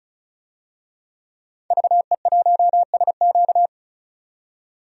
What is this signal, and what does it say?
Morse code, heard as a keyed tone, about 34 wpm, 700 Hz. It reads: VE1SQ
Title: DXCC-100 Call Signs - 5 Characters - 35wpm